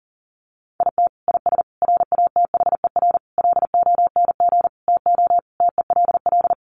Morse code, heard as a keyed tone, about 40 wpm, 700 Hz. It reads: IT IS RATHER LONG TO TELL
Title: Sentences from Top 700 Words 40wpm